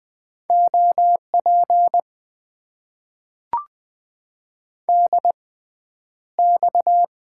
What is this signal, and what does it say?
Morse code, heard as a keyed tone, about 20 wpm, spaced as for 10 wpm, 700 Hz. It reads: OP DX
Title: Top 90 QSO Elements - 6x Character Spacing - 20wpm